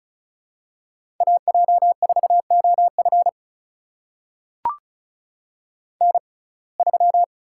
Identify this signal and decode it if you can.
Morse code, heard as a keyed tone, about 36 wpm, spaced as for 25 wpm, 700 Hz. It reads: AJ4OF N3
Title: USA Call Signs - 5 Characters - 6x Character Spacing - 35wpm